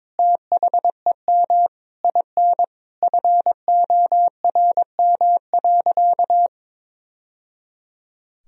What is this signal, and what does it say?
Morse code, heard as a keyed tone, about 22 wpm, 700 Hz. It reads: THEM IN FORM.
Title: Sentences from Top 1000 Words - Review Part 1 - 22wpm